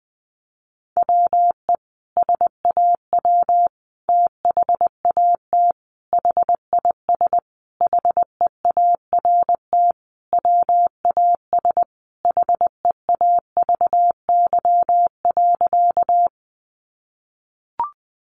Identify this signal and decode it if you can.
Morse code, heard as a keyed tone, about 20 wpm, 700 Hz. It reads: WE SAW THAT HIS HEART WAS HEAVY.
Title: Sentences from Top 1200 Words 20wpm